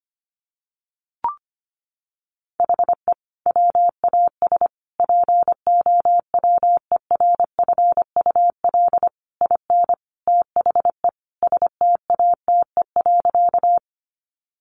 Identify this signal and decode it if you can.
Morse code, heard as a keyed tone, about 25 wpm, 700 Hz. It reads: HE WAS POWERFUL IN THE STATE.
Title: Sentences from Top 1300 Words 25wpm